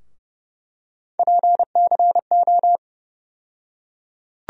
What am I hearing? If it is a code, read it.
Morse code, heard as a keyed tone, about 30 wpm, 700 Hz. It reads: PCO
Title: USA Call Sign Postfixes - 3 Letters 30wpm